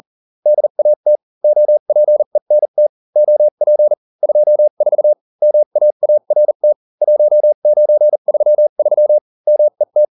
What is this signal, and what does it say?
Morse code, heard as a keyed tone, about 40 wpm, spaced as for 35 wpm, 600 Hz. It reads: DAT OPENT OP 24 MAART 1933 MET